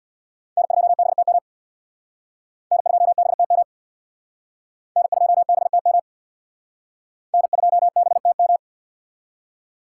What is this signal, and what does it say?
Morse code, heard as a keyed tone, about 65 wpm, 700 Hz. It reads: N2BTM N2BTM N2BTM N2BTM